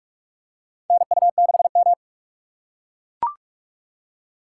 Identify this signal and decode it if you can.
Morse code, heard as a keyed tone, about 41 wpm, 700 Hz. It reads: NU6M
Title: USA Call Signs - 4 Characters - 4x Character Spacing - 45wpm